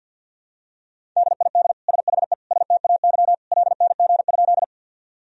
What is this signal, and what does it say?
Morse code, heard as a keyed tone, about 52 wpm, 700 Hz. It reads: DID SHE STAY LNG?
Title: Sentences from Top 500 Words and Abbreviations 50wpm